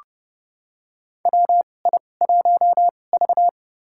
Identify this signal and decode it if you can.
Morse code, heard as a keyed tone, about 32 wpm, spaced as for 22 wpm, 700 Hz. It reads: WI1V
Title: USA Call Signs - 4 Characters - 2x Character Spacing - 30wpm